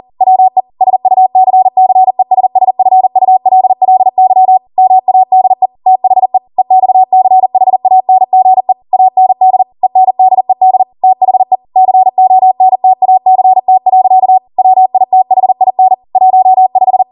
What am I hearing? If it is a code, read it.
Morse code, heard as a keyed tone, about 40 wpm, 750 Hz. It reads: WE SUCCESSFULLY MADE THE EXCHANGE AND ENDED THE CONTACT. WITHIN 15